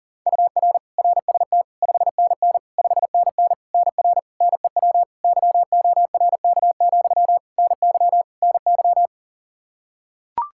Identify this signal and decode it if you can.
Morse code, heard as a keyed tone, about 40 wpm, 700 Hz. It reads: UR RST 5NN 5NN NR NEW YORK, NY NY
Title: Practice QSO - Exchange 1 - Beginning - 40wpm